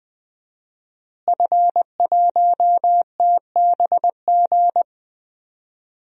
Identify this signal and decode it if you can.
Morse code, heard as a keyed tone, about 20 wpm, 700 Hz. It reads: F1TBG
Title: DXCC-100 Call Signs - 5 Characters - Repeated 3x - 20wpm